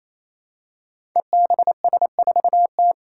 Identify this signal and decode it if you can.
Morse code, heard as a keyed tone, about 27 wpm, 700 Hz. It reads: EBS4T
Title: Five Challenging Letter-Number Combinations 28wpm